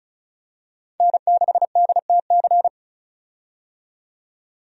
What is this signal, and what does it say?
Morse code, heard as a keyed tone, about 36 wpm, 700 Hz. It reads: N6DTC
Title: USA Call Signs - 5 Characters - 35wpm